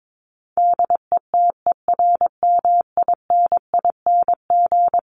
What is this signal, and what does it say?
Morse code, heard as a keyed tone, about 22 wpm, 700 Hz. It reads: DETERMINING
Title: ING Suffix - Top 100 Words with Base Words 22wpm